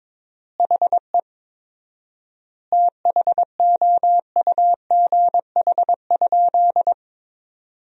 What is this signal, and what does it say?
Morse code, heard as a keyed tone, about 21 wpm, 700 Hz. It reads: HE THOUGH?
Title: 4x Word-Spacing - Sentences from Top 400 Words - 22wpm